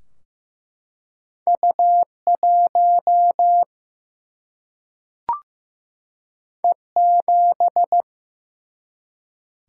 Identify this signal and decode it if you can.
Morse code, heard as a keyed tone, about 15 wpm, 700 Hz. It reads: U1 E7
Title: Single-Letter Single-Number 15wpm